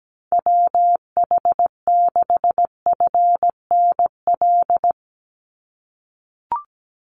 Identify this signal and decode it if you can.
Morse code, heard as a keyed tone, about 17 wpm, 700 Hz. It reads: WH6FNL
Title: USA Call Signs 17wpm